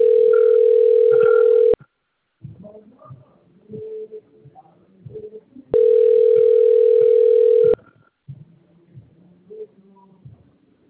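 Ringback tone (North America), 440 + 480 Hz.
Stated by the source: Outgoing telephone call recorded on a phone system labelled Bebedouro 1.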